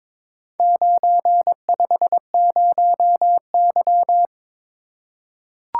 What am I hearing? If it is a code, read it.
Morse code, heard as a keyed tone, about 23 wpm, 700 Hz. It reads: 950Y
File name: practice / 22wpm / Four Letter-Number Combinations - 22wpm